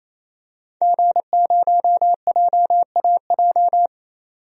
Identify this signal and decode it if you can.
Morse code, heard as a keyed tone, about 28 wpm, 700 Hz. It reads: G0JAJ
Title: DXCC-100 Call Signs - 5 Characters - 28wpm